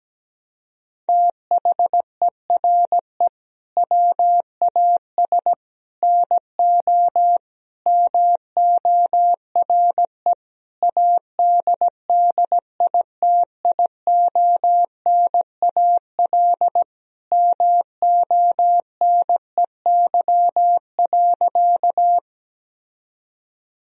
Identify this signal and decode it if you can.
Morse code, heard as a keyed tone, about 17 wpm, 700 Hz. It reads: THERE WAS NO MORE ADDITIONAL MONEY.
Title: Sentences from Top 1000 Words - Review Part 4 - 17wpm